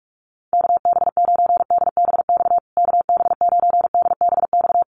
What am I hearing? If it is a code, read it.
Morse code, heard as a keyed tone, about 44 wpm, 700 Hz. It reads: KB9DBX KB9DBX
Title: USA Call Signs Repeated 3x 45wpm